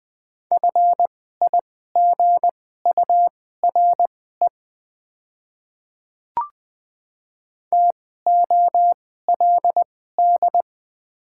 Morse code, heard as a keyed tone, about 20 wpm, spaced as for 14 wpm, 690 Hz. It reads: FIGURE TOLD
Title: 2x Character Spacing - Top 500 Words - 20wpm